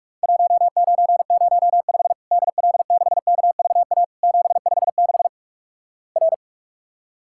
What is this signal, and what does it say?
Morse code, heard as a keyed tone, about 46 wpm, 700 Hz. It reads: 1905 DL6KVA 756 R
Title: Worked All Europe DX Contest - QTC Exchange - 45wpm